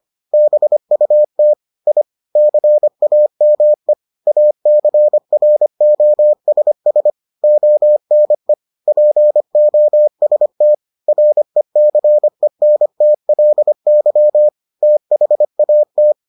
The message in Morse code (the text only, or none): BUT I CAME ACROSS ONE POST RECENTLY THAT